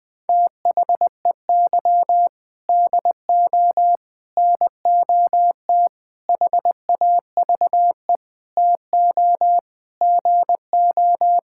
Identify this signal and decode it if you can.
Morse code, heard as a keyed tone, about 20 wpm, 700 Hz. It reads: THEY DO NOT HAVE TO GO
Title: Sentences with 5-Word Phrases - Spoken Beginning - 20wpm